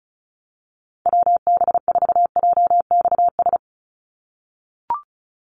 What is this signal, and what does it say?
Morse code, heard as a keyed tone, about 36 wpm, 700 Hz. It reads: WB4JXS
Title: USA Call Signs 35wpm